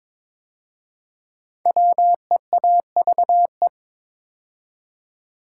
Morse code, heard as a keyed tone, about 22 wpm, 700 Hz. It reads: WEAVE
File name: 5 Letter Words - First 3 Letters Given - 22wpm